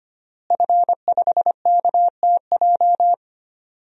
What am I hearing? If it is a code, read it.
Morse code, heard as a keyed tone, about 25 wpm, 700 Hz. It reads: F5KTJ
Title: DXCC-100 Call Signs - 5 Characters - Repeated 3x - 25wpm